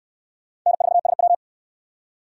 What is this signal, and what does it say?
Morse code, heard as a keyed tone, about 69 wpm, 700 Hz. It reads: N4SW